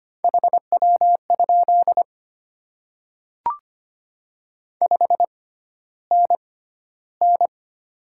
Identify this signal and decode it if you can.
Morse code, heard as a keyed tone, about 25 wpm, spaced as for 13 wpm, 700 Hz. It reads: HW? 5NN
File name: Top 20 QSO Elements - 6x Character Spacing - 25wpm